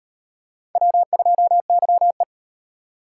Morse code, heard as a keyed tone, about 35 wpm, 700 Hz. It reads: W2YE